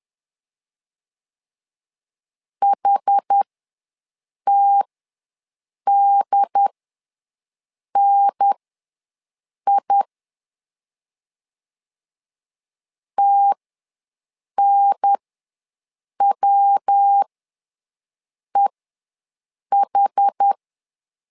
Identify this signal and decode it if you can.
Morse code, heard as a keyed tone, about 10 wpm, spaced as for 6 wpm, 800 Hz. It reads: HTDNI TNWEH
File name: LEZIONI / 3 MWRO / 3 MW ws 10wpm 2213